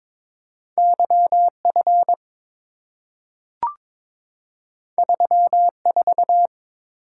Morse code, heard as a keyed tone, about 22 wpm, 700 Hz. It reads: YF 34
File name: practice / 22wpm / Two Letter-Number Combinations - 22wpm